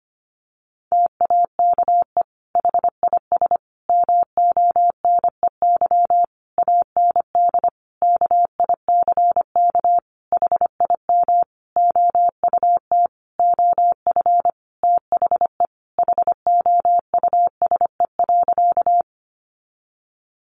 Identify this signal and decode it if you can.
Morse code, heard as a keyed tone, about 25 wpm, 700 Hz. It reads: TAKE HIS MONEY AND KICK HIM OUT OF THE HOUSE.
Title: Sentences from Top 1300 Words 25wpm